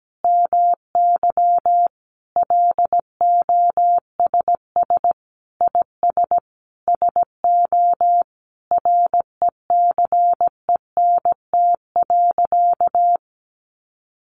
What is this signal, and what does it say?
Morse code, heard as a keyed tone, about 17 wpm, 700 Hz. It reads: MY LOSS IS SO RECENT.
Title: Sentences from Top 1400 Words 17wpm